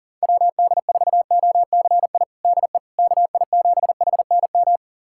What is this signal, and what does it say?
Morse code, heard as a keyed tone, about 40 wpm, 700 Hz. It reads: WD4OCI DE KI7HNM
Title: USA Callsign DE Callsign - 6 Characters - 40wpm